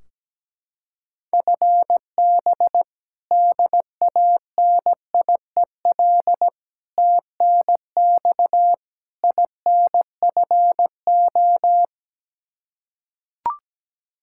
Morse code, heard as a keyed tone, about 17 wpm, 700 Hz. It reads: FB DANIEL TNX INFO
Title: Practice QSO - Exchange 3 - More Info - 17wpm